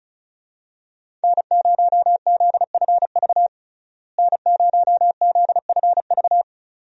Morse code, heard as a keyed tone, about 34 wpm, 700 Hz. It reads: N0ZFV N0ZFV
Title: USA Call Signs - 5 Characters - Repeated 2x - 35wpm